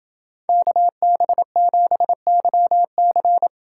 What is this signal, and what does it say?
Morse code, heard as a keyed tone, about 27 wpm, 700 Hz. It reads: KB7YC